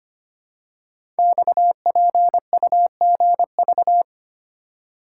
Morse code, heard as a keyed tone, about 25 wpm, 700 Hz. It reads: XPUGV
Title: Code Groups - 5 Letters - 25wpm